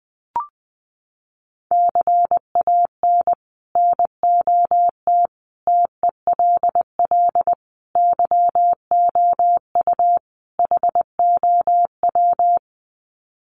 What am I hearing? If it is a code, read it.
Morse code, heard as a keyed tone, about 20 wpm, 700 Hz. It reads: CAN NOT TELL YOU HOW
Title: Sets of 5 Words Encore2 20wpm